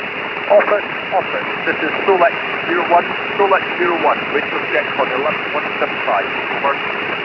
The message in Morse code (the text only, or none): none